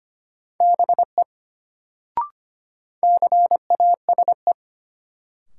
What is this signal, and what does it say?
Morse code, heard as a keyed tone, about 25 wpm, 700 Hz. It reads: BE CASE
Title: Sets of 2 Words - New Word Drill - 25wpm